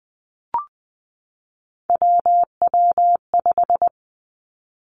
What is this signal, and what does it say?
Morse code, heard as a keyed tone, about 20 wpm, 700 Hz. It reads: WW5
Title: Three Challenging Letter-Number Combinations 20wpm